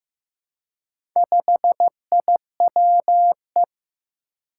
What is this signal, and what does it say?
Morse code, heard as a keyed tone, about 15 wpm, 700 Hz. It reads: 5IWE
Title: Four Challenging Letter-Number Combinations 15wpm